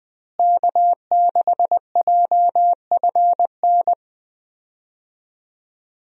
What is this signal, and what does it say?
Morse code, heard as a keyed tone, about 20 wpm, 700 Hz. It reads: K6JFN
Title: USA Call Signs - 5 Characters - Repeated 2x - 20wpm